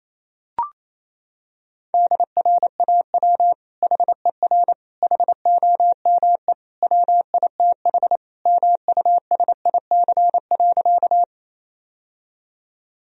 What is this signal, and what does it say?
Morse code, heard as a keyed tone, about 28 wpm, 700 Hz. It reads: DRAW HER HOME WITH MUSIC.
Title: Sentences from Top 1000 Words - Review Part 1 - 28wpm